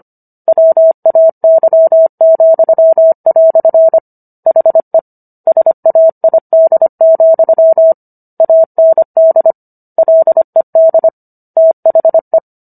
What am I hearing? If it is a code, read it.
Morse code, heard as a keyed tone, about 25 wpm, 650 Hz. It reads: WAY," HE SAID, AND LED THE